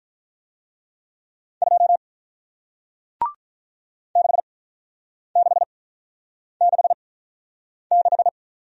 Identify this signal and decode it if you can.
Morse code, heard as a keyed tone, about 48 wpm, spaced as for 9 wpm, 700 Hz. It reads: 2 6666